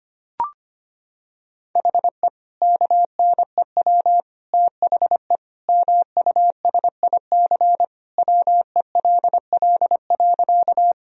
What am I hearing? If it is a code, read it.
Morse code, heard as a keyed tone, about 25 wpm, 700 Hz. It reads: HE KNEW THE MUSIC WELL.
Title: Sentences from Top 1000 Words - Review Part 1 - 25wpm